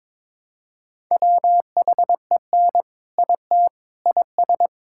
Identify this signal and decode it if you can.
Morse code, heard as a keyed tone, about 22 wpm, 700 Hz. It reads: WHEN IT IS
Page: morse-code-ninja-practice-22wpm